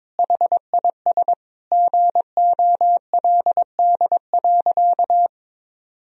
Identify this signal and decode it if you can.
Morse code, heard as a keyed tone, about 22 wpm, 700 Hz. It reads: HIS GOLD.
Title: Sentences from Top 1000 Words - Review Part 4 - 22wpm